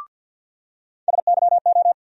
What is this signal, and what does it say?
Morse code, heard as a keyed tone, about 46 wpm, 700 Hz. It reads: SYO
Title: USA Call Sign Postfixes - 3 Letters 50wpm